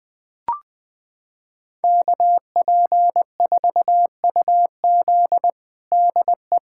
Morse code, heard as a keyed tone, about 21 wpm, 700 Hz. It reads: KP4UZ DE
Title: USA Callsign DE Callsign - 5 Characters - 20wpm